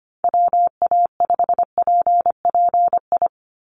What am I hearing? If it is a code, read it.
Morse code, heard as a keyed tone, about 25 wpm, 700 Hz. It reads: WA5PPI
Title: USA Call Signs 25wpm